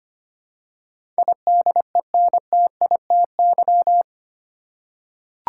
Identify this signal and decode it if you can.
Morse code, heard as a keyed tone, about 25 wpm, 700 Hz. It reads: IDENTITY
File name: Eight Letter Words - 25wpm